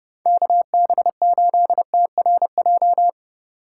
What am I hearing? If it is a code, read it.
Morse code, heard as a keyed tone, about 30 wpm, 700 Hz. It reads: KB8TRJ
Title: Speed Racing - USA 2x3 Call Signs 25wpm